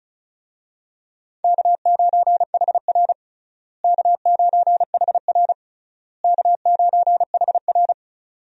Morse code, heard as a keyed tone, about 34 wpm, 700 Hz. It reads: K9HR K9HR K9HR
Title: USA Call Signs - 4 Characters - Repeated 3x - 35wpm